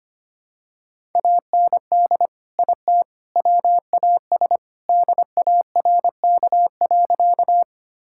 Morse code, heard as a keyed tone, about 25 wpm, 700 Hz. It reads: AND IT WAS DARK.
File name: practice / 25wpm / Sentences with 3-Word Phrases - 25wpm